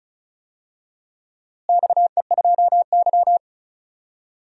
Morse code, heard as a keyed tone, about 34 wpm, 700 Hz. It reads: XE2Y
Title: DXCC-100 Call Signs - 4 Characters - Repeated 3x - 35wpm